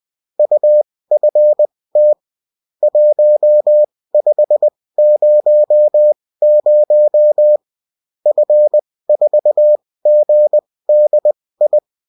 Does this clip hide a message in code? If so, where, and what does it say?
Yes: Morse code from 0.4 to 11.8 s, heard as a keyed tone, about 20 wpm, spaced as for 16 wpm, 600 Hz. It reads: UFT 1500 F4GDI